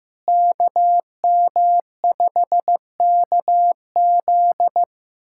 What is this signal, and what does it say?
Morse code, heard as a keyed tone, about 15 wpm, 700 Hz. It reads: KM5KZ
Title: USA Callsign DE Callsign - 5 Characters - 15wpm